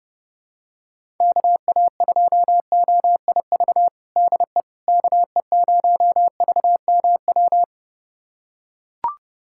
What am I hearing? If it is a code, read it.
Morse code, heard as a keyed tone, about 30 wpm, 700 Hz. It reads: KA2OIV DE KE0VMW
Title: USA Callsign DE Callsign - 6 Characters - 30wpm